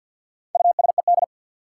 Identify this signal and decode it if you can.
Morse code, heard as a keyed tone, about 53 wpm, 700 Hz. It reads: USED
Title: Sets of 2 Words - New Word Drill - 50wpm